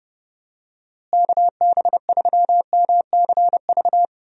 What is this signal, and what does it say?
Morse code, heard as a keyed tone, about 30 wpm, 700 Hz. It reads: KB3MCV
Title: USA Call Signs - 6 Characters - Repeated 2x - 30wpm